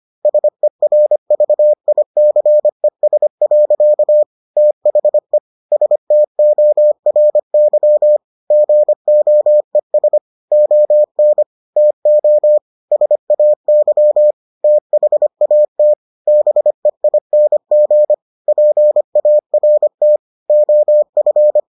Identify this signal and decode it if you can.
Morse code, heard as a keyed tone, about 25 wpm, 600 Hz. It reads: SERVICES. THE STORY GOES ON TO SAY THAT BEING PART OF